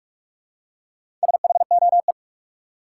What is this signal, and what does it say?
Morse code, heard as a keyed tone, about 46 wpm, 700 Hz. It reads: SHOE
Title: Four Letter Words - 45wpm